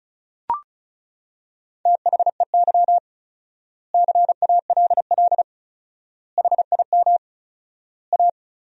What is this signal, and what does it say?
Morse code, heard as a keyed tone, about 34 wpm, 700 Hz. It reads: THEY CALL HIM A
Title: 4x Word-Spacing - Sentences from Top 500 Words - 35wpm